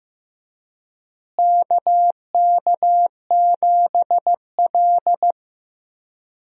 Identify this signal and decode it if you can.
Morse code, heard as a keyed tone, about 15 wpm, 700 Hz. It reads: KK7L
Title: USA Call Signs - 4 Characters - Repeated 3x - 15wpm